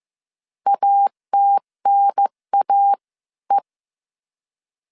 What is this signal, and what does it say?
Morse code, heard as a keyed tone, about 15 wpm, spaced as for 13 wpm, 800 Hz. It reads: ATNAE
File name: CW / DET / 1 3 1500C S C 14wpm 2458